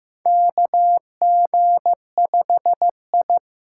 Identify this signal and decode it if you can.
Morse code, heard as a keyed tone, about 15 wpm, 700 Hz. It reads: KG5I
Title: USA Call Signs - 6x Character Spacing - 15wpm